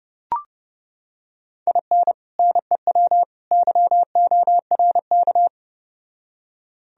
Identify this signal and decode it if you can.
Morse code, heard as a keyed tone, about 30 wpm, 700 Hz. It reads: IN NEW YORK